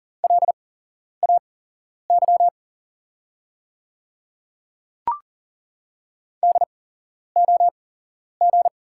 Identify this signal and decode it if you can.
Morse code, heard as a keyed tone, about 40 wpm, spaced as for 11 wpm, 700 Hz. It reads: LAY DOG
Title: 8x Character Spacing - Top 500 Words - 40wpm